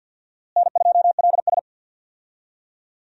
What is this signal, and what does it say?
Morse code, heard as a keyed tone, about 46 wpm, 700 Hz. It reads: N2LS
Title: USA Call Signs - 4 Characters - Repeated 2x - 50wpm